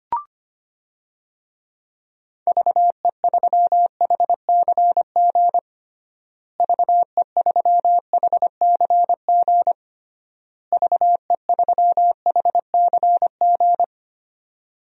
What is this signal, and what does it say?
Morse code, heard as a keyed tone, about 25 wpm, 700 Hz. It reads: VE3HCG VE3HCG VE3HCG
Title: DXCC-100 Call Signs - 6 Characters - Repeated 3x - 25wpm